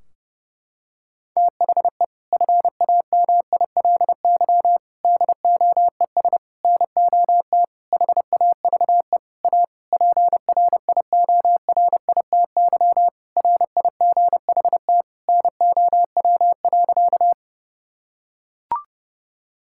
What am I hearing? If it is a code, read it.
Morse code, heard as a keyed tone, about 30 wpm, 700 Hz. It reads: THE FAMILY DOES NOT HAVE A PRIORITY RIGHT NOW.
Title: Sentences from Top 1600 Words 30wpm